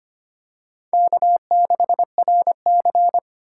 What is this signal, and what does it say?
Morse code, heard as a keyed tone, about 25 wpm, 700 Hz. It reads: K6RC
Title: K1USN SST QSO Practice - 25wpm